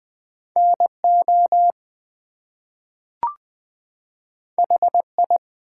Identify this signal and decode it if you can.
Morse code, heard as a keyed tone, about 20 wpm, 700 Hz. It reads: NO HI
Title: Two Letter Words - 20wpm